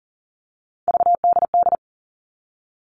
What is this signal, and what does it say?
Morse code, heard as a keyed tone, about 40 wpm, 700 Hz. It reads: VDD
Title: All Cut Numbers - 3-Digit 40wpm